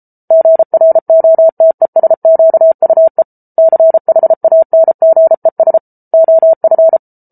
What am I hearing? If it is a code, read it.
Morse code, heard as a keyed tone, about 34 wpm, 650 Hz. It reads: GROTESQUE CHANGES OF